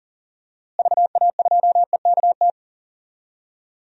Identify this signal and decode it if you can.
Morse code, heard as a keyed tone, about 40 wpm, 700 Hz. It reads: VA2EKT
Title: DXCC-100 Call Signs - 6 Characters - Repeated 2x - 40wpm